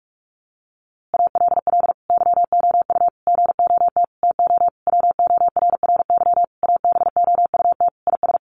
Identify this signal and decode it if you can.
Morse code, heard as a keyed tone, about 44 wpm, 700 Hz. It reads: ALL YOU GOT TO WORRY ABOUT IS
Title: A Sound Of Thunder 45wpm